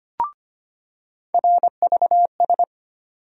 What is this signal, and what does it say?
Morse code, heard as a keyed tone, about 25 wpm, 700 Hz. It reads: RVS
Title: USA Call Sign Postfixes - 3 Letters 25wpm